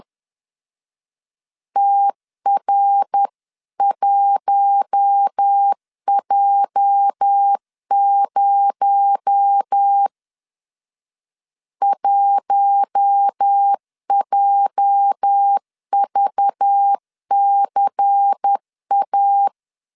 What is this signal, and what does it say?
Morse code, heard as a keyed tone, about 10 wpm, 800 Hz. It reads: TR1J0 1JVCA
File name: LEZIONI / 8 YJ45 / 8 YJ 10wpm 2651